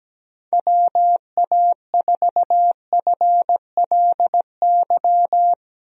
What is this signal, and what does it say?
Morse code, heard as a keyed tone, about 17 wpm, 700 Hz. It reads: WA4FLY
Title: USA Call Signs - 6 Characters - 17wpm